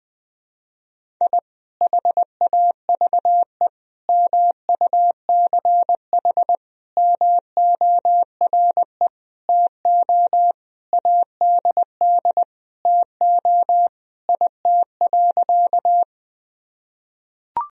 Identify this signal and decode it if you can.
Morse code, heard as a keyed tone, about 21 wpm, 700 Hz. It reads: I HAVE MUCH MORE TO ADD TO IT.
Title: Sentences from Top 1000 Words - Review Part 1 - 20wpm